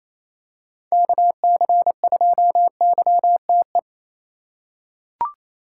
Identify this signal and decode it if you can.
Morse code, heard as a keyed tone, about 28 wpm, 700 Hz. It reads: KC2YTE
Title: USA Call Signs - 6 Characters - 6x Character Spacing - 28wpm